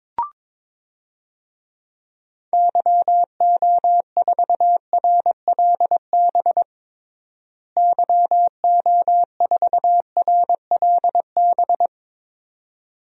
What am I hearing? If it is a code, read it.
Morse code, heard as a keyed tone, about 22 wpm, 700 Hz. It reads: YO4RLB YO4RLB
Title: DXCC-100 Call Signs - 6 Characters - Repeated 2x - 22wpm